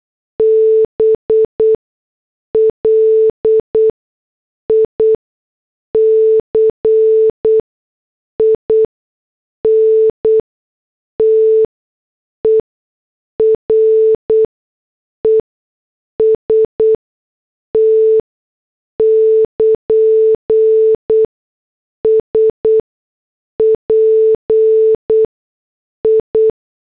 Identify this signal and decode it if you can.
Morse code, heard as a keyed tone, about 8 wpm, spaced as for 6 wpm, 440 Hz. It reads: BLIC INTEREST <KN> SPI